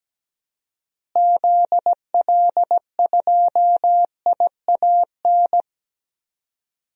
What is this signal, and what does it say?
Morse code, heard as a keyed tone, about 17 wpm, 700 Hz. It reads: ZL2IAN
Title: DXCC-100 Call Signs - 6 Characters - Repeated 3x - 17wpm